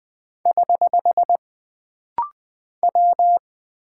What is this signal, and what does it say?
Morse code, heard as a keyed tone, about 20 wpm, 700 Hz. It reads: <HH> W
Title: Single Character (Letter:Number:Punctuation:Prosign) - Rapid-Fire - 20wpm